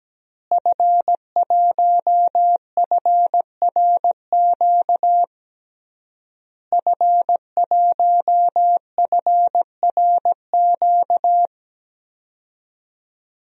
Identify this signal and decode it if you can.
Morse code, heard as a keyed tone, about 17 wpm, 700 Hz. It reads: F1FRQ F1FRQ
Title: DXCC-100 Call Signs - 5 Characters - Repeated 3x - 17wpm